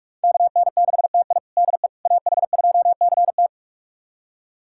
Morse code, heard as a keyed tone, about 46 wpm, 700 Hz. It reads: KN6TI DE AH2CT
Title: USA Callsign DE Callsign - 5 Characters - 45wpm